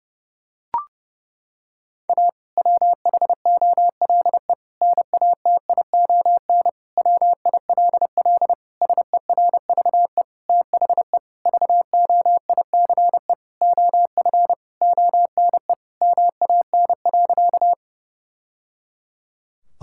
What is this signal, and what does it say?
Morse code, heard as a keyed tone, about 32 wpm, 700 Hz. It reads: A WHOLE NATION WILL SERVE THE VOICE OF ONE MAN.
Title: Sentences from Top 1000 Words - Review Part 6 - 30wpm